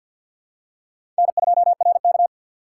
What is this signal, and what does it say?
Morse code, heard as a keyed tone, about 51 wpm, 700 Hz. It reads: N2RK
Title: USA Call Signs - 4 Characters - 2x Character Spacing - 50wpm